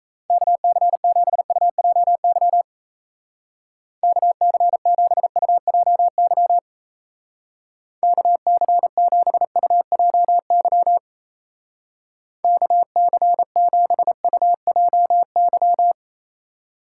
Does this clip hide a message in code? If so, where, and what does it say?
Yes: Morse code from 0.3 to 15.9 s, heard as a keyed tone, about 36 wpm, 700 Hz. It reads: KC7UJY KC7UJY KC7UJY KC7UJY